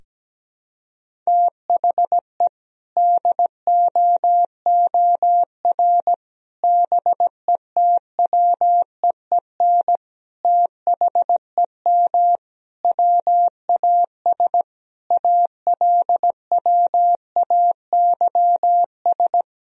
Morse code, heard as a keyed tone, about 17 wpm, 700 Hz. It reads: THE DOOR BETWEEN THEM WAS ALWAYS
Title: Sentences from Top 1000 Words - Review Part 1 - 17wpm